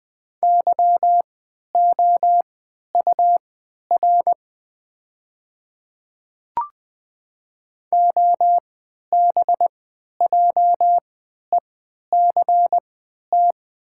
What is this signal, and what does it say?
Morse code, heard as a keyed tone, about 20 wpm, spaced as for 11 wpm, 700 Hz. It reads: YOUR OBJECT
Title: 3x Character Spacing - Top 500 Words - 20wpm